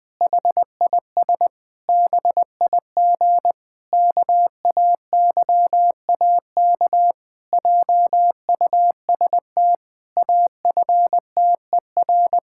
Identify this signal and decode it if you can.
Morse code, heard as a keyed tone, about 20 wpm, 700 Hz. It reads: HIS BIG KAYAK JUST AFTER
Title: Pangram Challenge v2 - 20wpm